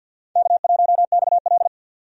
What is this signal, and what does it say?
Morse code, heard as a keyed tone, about 52 wpm, 700 Hz. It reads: K1XL
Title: USA Call Signs - 4 Characters - 50wpm